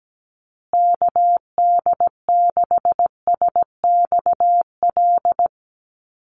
Morse code, heard as a keyed tone, about 17 wpm, 700 Hz. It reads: KD6SXL
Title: USA Call Signs Repeated 2x 17wpm